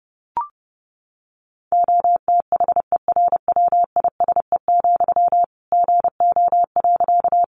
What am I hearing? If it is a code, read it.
Morse code, heard as a keyed tone, about 30 wpm, 700 Hz. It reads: OTHERWISE, GO.
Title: Sentences from Top 1200 Words 30wpm